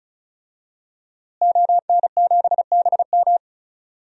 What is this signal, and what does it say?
Morse code, heard as a keyed tone, about 36 wpm, 700 Hz. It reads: ON7BM
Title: DXCC-100 Call Signs - 5 Characters - Repeated 3x - 35wpm